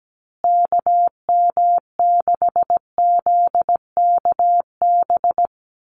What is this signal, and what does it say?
Morse code, heard as a keyed tone, about 17 wpm, 700 Hz. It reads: KM6ZKB
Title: USA Call Signs Repeated 2x 17wpm